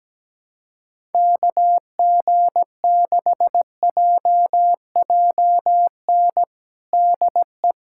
Morse code, heard as a keyed tone, about 17 wpm, 700 Hz. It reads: KG6JJN DE
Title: USA Callsign DE Callsign - 6 Characters - 17wpm